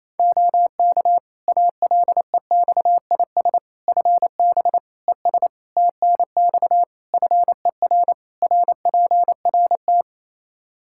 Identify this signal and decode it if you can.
Morse code, heard as a keyed tone, about 28 wpm, 700 Hz. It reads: OK ALEXIS FB ES TNX FER RPRT
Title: Practice QSO - Exchange 2 - Info - 28wpm